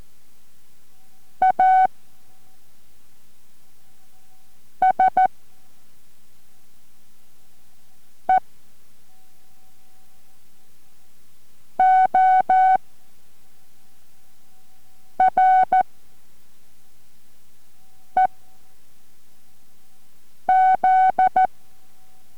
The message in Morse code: ASEOREZ